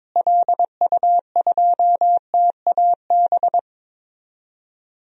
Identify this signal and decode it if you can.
Morse code, heard as a keyed tone, about 22 wpm, 700 Hz. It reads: LU2TAB
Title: DXCC-100 Call Signs - 6 Characters - Repeated 2x - 22wpm